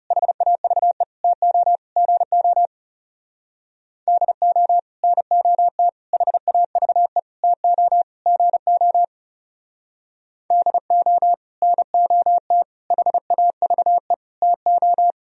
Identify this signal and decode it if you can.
Morse code, heard as a keyed tone, about 39 wpm, 700 Hz. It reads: HAVE TO GO DO NOT HAVE TO GO DO NOT HAVE TO